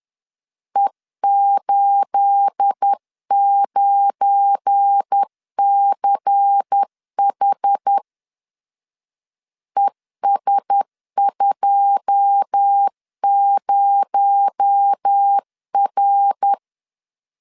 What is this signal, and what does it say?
Morse code, heard as a keyed tone, about 10 wpm, 800 Hz. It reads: E89CH ES20R
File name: LEZIONI / 9 28 10wpm 2842